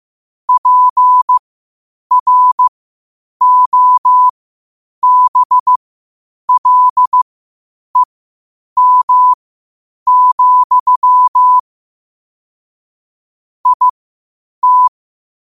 Morse code, heard as a keyed tone, about 15 wpm, spaced as for 8 wpm, 1000 Hz. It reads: PROBLEM, IT
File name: qotdfiles / 8wpm